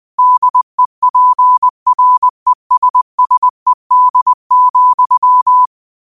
Morse code, heard as a keyed tone, about 20 wpm, 1000 Hz. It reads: DEPRESSED,
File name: 20wpm